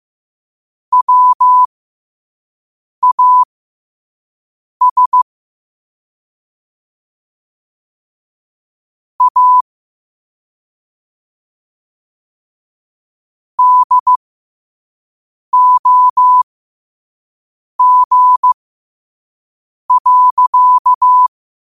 Morse code, heard as a keyed tone, about 15 wpm, spaced as for 5 wpm, 1000 Hz. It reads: WAS A DOG.
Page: New quotes every day in morse code at 5 Words per minute.